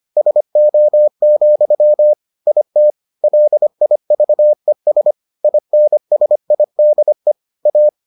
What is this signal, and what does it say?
Morse code, heard as a keyed tone, about 25 wpm, 600 Hz. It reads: SO, IT LIVES INSIDE A